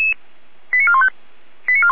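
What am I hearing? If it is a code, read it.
Five-tone selective call (ZVEI), heard as quick runs of five tones, tones of about 70 ms. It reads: 89415
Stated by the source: received on an SDR; ZVEI five-tone call 89187, 455 directory.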